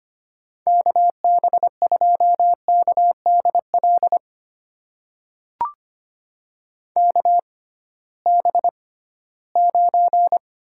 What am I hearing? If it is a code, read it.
Morse code, heard as a keyed tone, about 25 wpm, spaced as for 16 wpm, 700 Hz. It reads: KB2KDL KB9
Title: USA Call Signs - 6 Characters - 6x Character Spacing - 25wpm